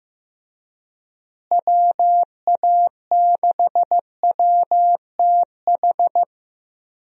Morse code, heard as a keyed tone, about 15 wpm, 700 Hz. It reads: WA6WTH
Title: USA Call Signs - 6 Characters - Repeated 3x - 15wpm